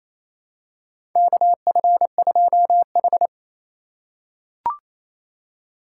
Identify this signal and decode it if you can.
Morse code, heard as a keyed tone, about 28 wpm, 700 Hz. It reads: KF2H
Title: USA Call Signs - 4 Characters - 4x Character Spacing - 28wpm